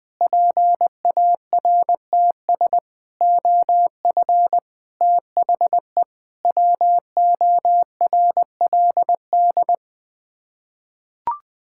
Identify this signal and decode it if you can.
Morse code, heard as a keyed tone, about 20 wpm, 700 Hz. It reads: PARTS OF THE WORLD
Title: Sets of 6 Words - 20wpm